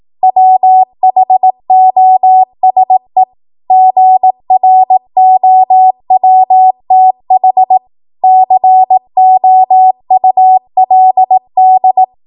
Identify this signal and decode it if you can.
Morse code, heard as a keyed tone, about 18 wpm, 750 Hz. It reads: WHOSE GROWTH COULD
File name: Morse / 18WPM